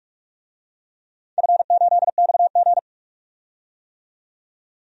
Morse code, heard as a keyed tone, about 47 wpm, 700 Hz. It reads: F8XG
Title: DXCC-100 Call Signs - 4 Characters - 45wpm